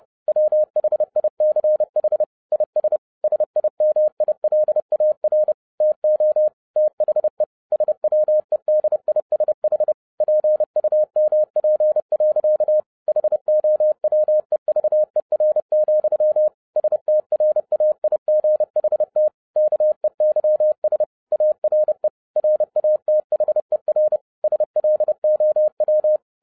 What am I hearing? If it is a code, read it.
Morse code, heard as a keyed tone, about 30 wpm, 610 Hz. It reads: WHICH IS SIMILAR TO THE SWEDISH PUMP. HOWEVER, STRAIGHT KEYS ARE RATHER SLOW
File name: 2011-07-mp3 / n20 GREETINGS FROM SINGAPORE 30wpm